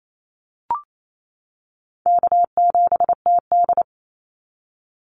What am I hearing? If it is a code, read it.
Morse code, heard as a keyed tone, about 28 wpm, 700 Hz. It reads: K7TD
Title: Call Signs - 28wpm